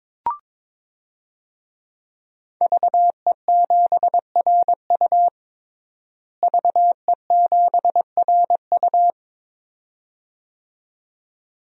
Morse code, heard as a keyed tone, about 22 wpm, 700 Hz. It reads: VE7RU VE7RU
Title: DXCC-100 Call Signs - 5 Characters - Repeated 2x - 22wpm